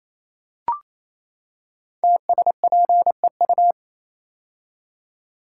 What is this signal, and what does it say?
Morse code, heard as a keyed tone, about 28 wpm, 700 Hz. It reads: TSPEU
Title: Code Groups - 5 Letter:Numbers - 28wpm